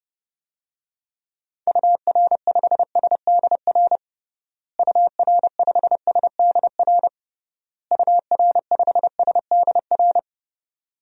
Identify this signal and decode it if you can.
Morse code, heard as a keyed tone, about 32 wpm, 700 Hz. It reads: UR5SDR UR5SDR UR5SDR
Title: DXCC-100 Call Signs - 6 Characters - Repeated 3x - 30wpm